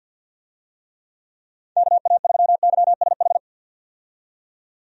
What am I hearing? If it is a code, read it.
Morse code, heard as a keyed tone, about 51 wpm, 700 Hz. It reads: KA3YSH
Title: USA Call Signs - 6 Characters - 50wpm